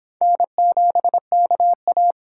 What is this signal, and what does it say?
Morse code, heard as a keyed tone, about 25 wpm, 700 Hz. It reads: N7KA